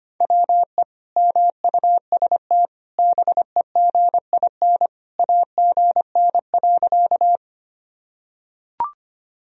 Morse code, heard as a keyed tone, about 25 wpm, 700 Hz. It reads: WE MUST BEGIN AGN.
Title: Sentences from Top 300 Words and Abbreviations 25wpm